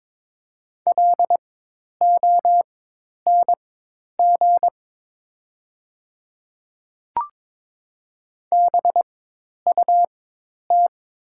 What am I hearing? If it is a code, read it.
Morse code, heard as a keyed tone, about 22 wpm, spaced as for 10 wpm, 700 Hz. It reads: LONG BUT
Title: 4x Character Spacing - Top 100 Words - 22wpm